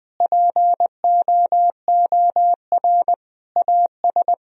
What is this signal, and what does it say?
Morse code, heard as a keyed tone, about 20 wpm, 700 Hz. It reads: POOR AS
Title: Sentences from Top 1000 Words - Review Part 2 - 20wpm